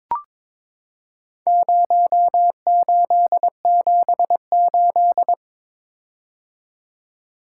Morse code, heard as a keyed tone, about 22 wpm, 700 Hz. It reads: 0878
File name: practice / 22wpm / Four Letter-Number Combinations - 22wpm